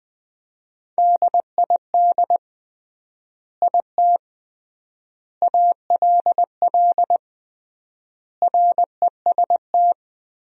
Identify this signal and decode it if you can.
Morse code, heard as a keyed tone, about 20 wpm, 700 Hz. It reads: DID IT ALL REST
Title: 3x Word-Spacing - Sentences from Top 500 Words - 20wpm